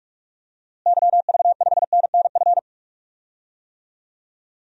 Morse code, heard as a keyed tone, about 46 wpm, 700 Hz. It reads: YV5NNF
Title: DXCC-100 Call Signs - 6 Characters - Repeated 3x - 45wpm